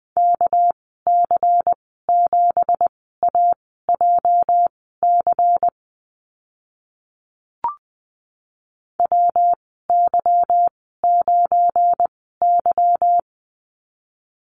Morse code, heard as a keyed tone, about 20 wpm, spaced as for 15 wpm, 700 Hz. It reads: KC7AJC WY9Y
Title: 219 Lesson 35g - Number 0 - Call Signs 20wpm